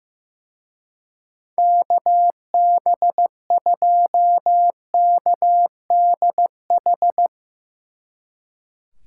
Morse code, heard as a keyed tone, about 15 wpm, 700 Hz. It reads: KB2KDH
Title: Speed Racing - USA 2x3 Call Signs 15wpm